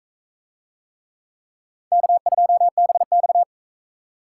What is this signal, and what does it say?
Morse code, heard as a keyed tone, about 39 wpm, 700 Hz. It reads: K2BX